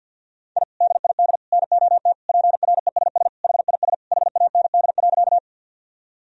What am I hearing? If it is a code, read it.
Morse code, heard as a keyed tone, about 52 wpm, 700 Hz. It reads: I DID NOT PRESS HIS HAND.
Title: Sentences from Top 1000 Words - Review Part 1 - 50wpm